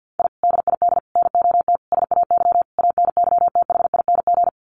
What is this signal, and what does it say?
Morse code, heard as a keyed tone, about 50 wpm, 700 Hz. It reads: I DID NOT SAY ANYTHING
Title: Sets of 5 Words Encore2 50wpm